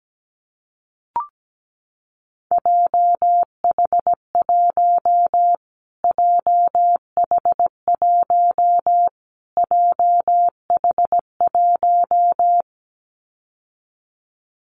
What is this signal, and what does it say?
Morse code, heard as a keyed tone, about 17 wpm, 700 Hz. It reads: JH1 JH1 JH1
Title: Callsign Prefixes for Top 25 DXCC - Repeated 3x - 17wpm